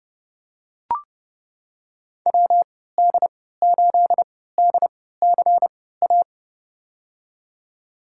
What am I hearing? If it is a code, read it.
Morse code, heard as a keyed tone, about 30 wpm, spaced as for 17 wpm, 700 Hz. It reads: WD8DCA
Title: USA Call Signs - 6 Characters - 3x Character Spacing - 30wpm